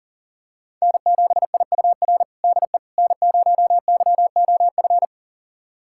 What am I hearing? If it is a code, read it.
Morse code, heard as a keyed tone, about 40 wpm, 700 Hz. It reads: N7IUR DE N0YOF
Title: USA Callsign DE Callsign - 5 Characters - 40wpm